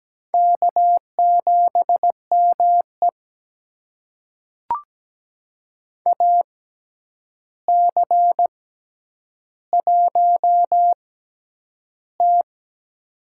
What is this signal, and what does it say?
Morse code, heard as a keyed tone, about 17 wpm, spaced as for 9 wpm, 700 Hz. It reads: K7ME AC1T
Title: USA Call Signs - 4 Characters - 6x Character Spacing - 17wpm